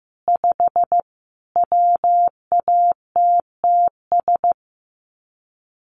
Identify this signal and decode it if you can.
Morse code, heard as a keyed tone, about 15 wpm, 700 Hz. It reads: 5 WATTS
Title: Common QSO Phrases 15wpm